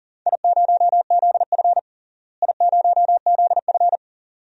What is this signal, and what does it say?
Morse code, heard as a keyed tone, about 40 wpm, 700 Hz. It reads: I0ZF I0ZF
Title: DXCC-100 Call Signs - 4 Characters - Repeated 2x - 40wpm